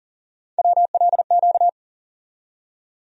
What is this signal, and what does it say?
Morse code, heard as a keyed tone, about 40 wpm, 700 Hz. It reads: WLQ
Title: USA Call Sign Postfixes - 3 Letters 40wpm